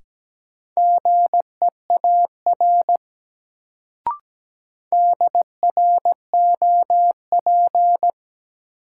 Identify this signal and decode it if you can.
Morse code, heard as a keyed tone, about 17 wpm, 700 Hz. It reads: GEAR DROP
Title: Sentences from Top 1400 Words - New Word Drill - 17wpm